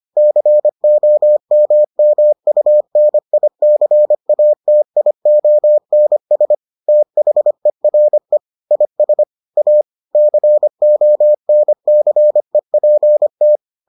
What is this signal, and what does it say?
Morse code, heard as a keyed tone, about 25 wpm, 600 Hz. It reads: COMMUNICATIONS THERE IS A CONCEPT